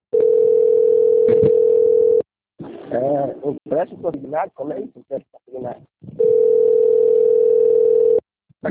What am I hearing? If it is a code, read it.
Ringback tone (North America), 440 + 480 Hz, with speech.
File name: out- - - - - .48461